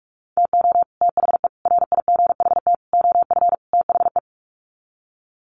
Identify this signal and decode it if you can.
Morse code, heard as a keyed tone, about 44 wpm, 700 Hz. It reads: TO THE RIGHT OF THE
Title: Sets of 5 Words Encore 45wpm